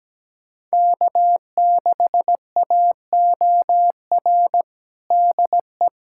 Morse code, heard as a keyed tone, about 17 wpm, 700 Hz. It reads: K6AOR DE
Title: USA Callsign DE Callsign 17wpm